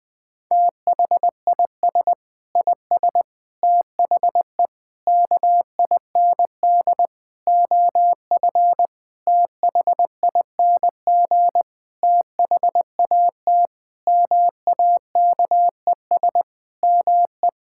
Morse code, heard as a keyed tone, about 20 wpm, 700 Hz. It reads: THIS IS THE KIND OF THING THAT MAKES ME
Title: Sentences with 7-Word Phrases - Spoken Beginning - 20wpm